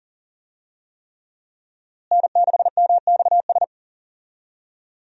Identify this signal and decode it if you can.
Morse code, heard as a keyed tone, about 40 wpm, 700 Hz. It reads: N6MXS
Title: USA Call Signs - 5 Characters - Repeated 3x - 40wpm